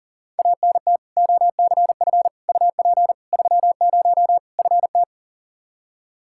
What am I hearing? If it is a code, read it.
Morse code, heard as a keyed tone, about 40 wpm, 700 Hz. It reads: ANT OCF UP 30 FT
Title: Practice QSO - Exchange 2 - Info - 40wpm